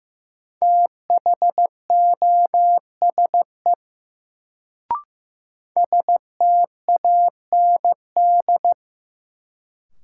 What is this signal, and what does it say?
Morse code, heard as a keyed tone, about 15 wpm, 700 Hz. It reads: THOSE STAND
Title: Sentences from Top 300 Words - New Word Drill - 15wpm